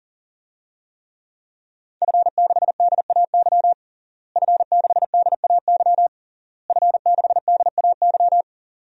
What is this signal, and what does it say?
Morse code, heard as a keyed tone, about 40 wpm, 700 Hz. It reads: F6DAY F6DAY F6DAY
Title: DXCC-100 Call Signs - 5 Characters - Repeated 3x - 40wpm